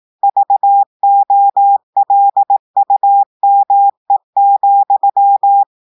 Morse code, heard as a keyed tone, about 18 wpm, 800 Hz. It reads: VOLUME,
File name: audios / portugais / portugais texte en clair 1 vitesse 18 mots minutes